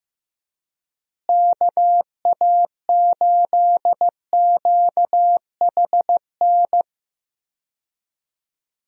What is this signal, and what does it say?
Morse code, heard as a keyed tone, about 15 wpm, 700 Hz. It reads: KA8QHN
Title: USA Call Signs - 6 Characters - Repeated 3x - 15wpm